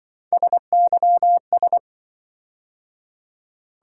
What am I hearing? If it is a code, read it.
Morse code, heard as a keyed tone, about 25 wpm, 700 Hz. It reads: SYS